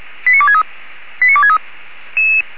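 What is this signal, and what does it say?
Five-tone selective call (ZVEI), heard as quick runs of five tones, tones of about 70 ms. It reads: 89263 89263
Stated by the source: ZVEI five-tone selective call received on SDR.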